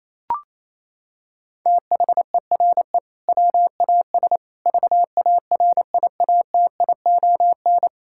Morse code, heard as a keyed tone, about 28 wpm, 700 Hz. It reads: THERE WAS VARIATION
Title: Sentences from Top 1600 Words 28wpm